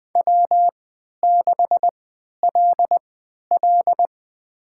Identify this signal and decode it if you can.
Morse code, heard as a keyed tone, about 20 wpm, spaced as for 11 wpm, 700 Hz. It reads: W6LL
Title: USA Call Signs - 4 Characters - 3x Character Spacing - 20wpm